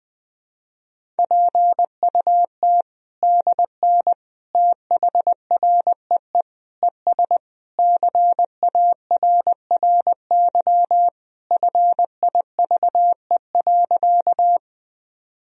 Morse code, heard as a keyed tone, about 21 wpm, 700 Hz. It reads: PUT DN THREE ES CARRY FIVE.
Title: Sentences from Top 400 Words and Abbreviations 20wpm